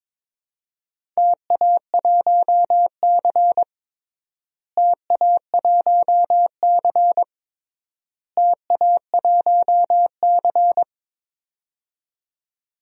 Morse code, heard as a keyed tone, about 21 wpm, 700 Hz. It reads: TA1C TA1C TA1C
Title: DXCC-100 Call Signs - 4 Characters - Repeated 3x - 22wpm